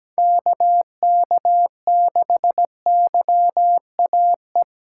Morse code, heard as a keyed tone, about 20 wpm, 700 Hz. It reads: KK6YAE
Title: Speed Racing - USA 2x3 Call Signs 17wpm